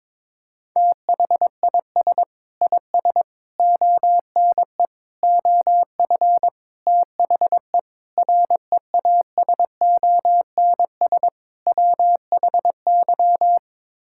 Morse code, heard as a keyed tone, about 22 wpm, 700 Hz. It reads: THIS IS ONE OF THE REASONS WHY
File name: practice / 22wpm / Sets of 7 Words - 22wpm